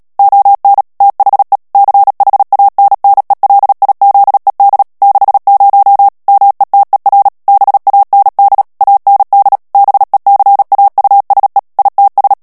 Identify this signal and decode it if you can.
Morse code, heard as a keyed tone, about 37 wpm, 800 Hz. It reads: ON THE CHANNELIZED 60 METER BAND AND BECAUSE ITS
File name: morse mp3 / ANGLAIS texte clair 3 vitesse 35 mots minute